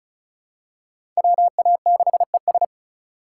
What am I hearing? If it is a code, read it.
Morse code, heard as a keyed tone, about 36 wpm, 700 Hz. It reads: WA6ES
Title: USA Call Signs - 5 Characters - 3x Character Spacing - 35wpm